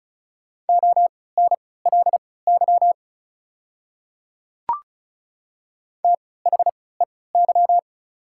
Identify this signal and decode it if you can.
Morse code, heard as a keyed tone, about 34 wpm, spaced as for 20 wpm, 690 Hz. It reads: ONLY THEY
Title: 3x Character Spacing - Top 500 Words - 35wpm